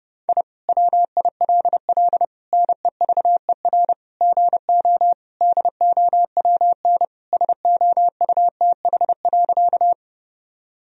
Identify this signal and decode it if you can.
Morse code, heard as a keyed tone, about 30 wpm, 700 Hz. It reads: I WILL NEVER GO DOWN SOUTH.
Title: Sentences from Top 1000 Words - Review Part 2 - 30wpm